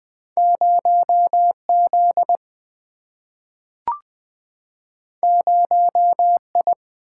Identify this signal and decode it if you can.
Morse code, heard as a keyed tone, about 20 wpm, 700 Hz. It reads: 0Z 0I
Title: Single-Letter Single-Number 20wpm